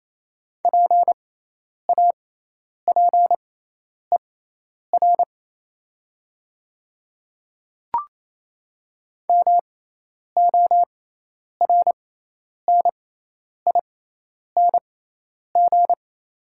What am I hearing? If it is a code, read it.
Morse code, heard as a keyed tone, about 28 wpm, spaced as for 10 wpm, 700 Hz. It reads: PAPER MORNING
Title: 6x Character Spacing - Top 500 Words - 28wpm